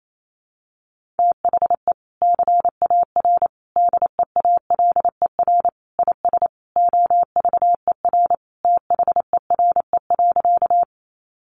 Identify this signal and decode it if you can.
Morse code, heard as a keyed tone, about 28 wpm, 700 Hz. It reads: THE CAR DEALER IS OVER THERE.
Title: Sentences from Top 1100 Words 28wpm